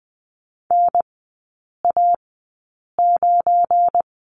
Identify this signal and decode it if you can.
Morse code, heard as a keyed tone, about 20 wpm, spaced as for 8 wpm, 690 Hz. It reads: NA9
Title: 212 Lesson 34e - Number 8 - Three Characters 20wpm